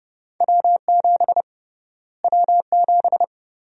Morse code, heard as a keyed tone, about 32 wpm, 700 Hz. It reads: W7 W7
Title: USA Call Sign Prefixes - Repeated 2x - 30wpm